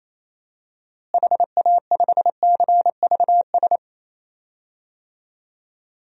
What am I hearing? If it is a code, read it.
Morse code, heard as a keyed tone, about 27 wpm, 700 Hz. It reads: HA5CVS
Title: DXCC-100 Call Signs - 6 Characters - Repeated 2x - 28wpm